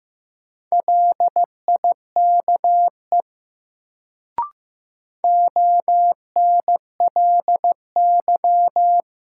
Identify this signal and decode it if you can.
Morse code, heard as a keyed tone, about 15 wpm, 700 Hz. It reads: LIKE ONLY
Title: Sets of 2 Words - New Word Drill - 15wpm